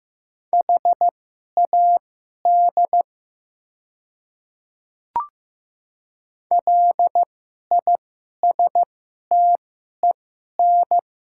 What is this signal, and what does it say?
Morse code, heard as a keyed tone, about 15 wpm, spaced as for 11 wpm, 700 Hz. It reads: HAD LISTEN
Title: 2x Character Spacing - Top 500 Words - 15wpm